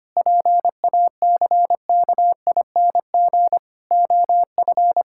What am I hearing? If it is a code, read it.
Morse code, heard as a keyed tone, about 25 wpm, 700 Hz. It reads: PACKING OF
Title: Pangram Challenge v2 - 25wpm